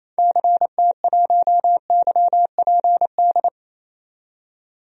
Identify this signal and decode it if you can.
Morse code, heard as a keyed tone, about 28 wpm, 700 Hz. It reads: CT1YPD
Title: DXCC-100 Call Signs - 6 Characters - 28wpm